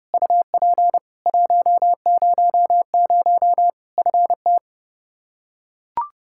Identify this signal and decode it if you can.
Morse code, heard as a keyed tone, about 30 wpm, 700 Hz. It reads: UP 100 FT
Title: Practice QSO - Exchange 2 - Info - 30wpm